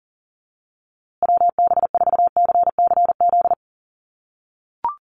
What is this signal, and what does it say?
Morse code, heard as a keyed tone, about 40 wpm, 700 Hz. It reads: WB4CCZ
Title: USA Call Signs Repeated 2x 40wpm